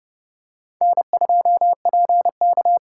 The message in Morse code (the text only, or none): N2PK